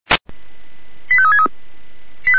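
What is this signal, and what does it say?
Five-tone selective call (ZVEI), heard as quick runs of five tones, tones of about 70 ms. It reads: 85363
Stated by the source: ZVEI five-tone selective call received on an SDR.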